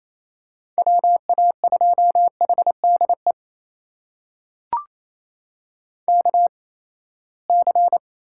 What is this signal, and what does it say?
Morse code, heard as a keyed tone, about 28 wpm, spaced as for 19 wpm, 700 Hz. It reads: WA2HDE KC
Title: USA Call Signs - 6 Characters - 8x Character Spacing - 28wpm